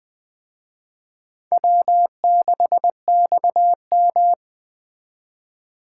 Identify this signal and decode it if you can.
Morse code, heard as a keyed tone, about 20 wpm, 700 Hz. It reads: W6XM
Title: Speed Racing - USA 1x2 Call Signs 17wpm